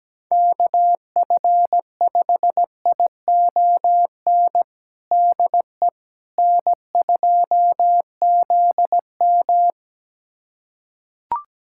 Morse code, heard as a keyed tone, about 17 wpm, 700 Hz. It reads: KF5ION DE N2ZM
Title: USA Callsign DE Callsign 17wpm